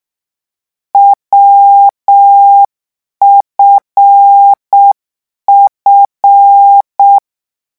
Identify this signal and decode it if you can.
Morse code, heard as a keyed tone, about 6 wpm, 800 Hz. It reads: WFF